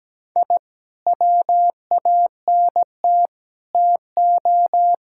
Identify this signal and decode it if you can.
Morse code, heard as a keyed tone, about 17 wpm, 700 Hz. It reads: I WANT TO
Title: Sentences from Top 1000 Words - Review Part 6 - 17wpm